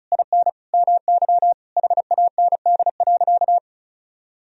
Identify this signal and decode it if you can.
Morse code, heard as a keyed tone, about 36 wpm, 700 Hz. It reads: IN MY HAND.
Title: Sentences from Top 1000 Words - Review Part 1 - 35wpm